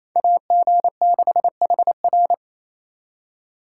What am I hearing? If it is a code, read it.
Morse code, heard as a keyed tone, about 33 wpm, 700 Hz. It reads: AG6HR
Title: Speed Racing - USA 2x2 Call Signs 28wpm